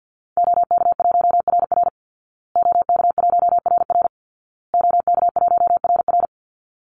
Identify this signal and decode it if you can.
Morse code, heard as a keyed tone, about 50 wpm, 700 Hz. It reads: OK1RR OK1RR OK1RR
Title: Call Signs Repeated 2x - 50wpm